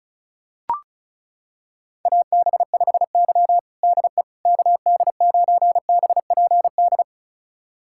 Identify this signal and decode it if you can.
Morse code, heard as a keyed tone, about 36 wpm, 690 Hz. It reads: AB5Y DE KD9BPD
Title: USA Callsign DE Callsign 35wpm